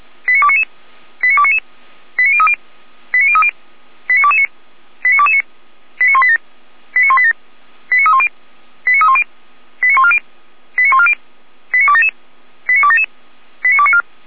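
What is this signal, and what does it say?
Five-tone selective call (ZVEI), heard as quick runs of five tones, tones of about 70 ms. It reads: 89299 89299 89030 89030 89229 89229 89187 89187 89310 89310 89140 89140 89277 89277 89294
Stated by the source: ZVEI five-tone call 18916 received by SDR, Feuerwehr Orlamünde.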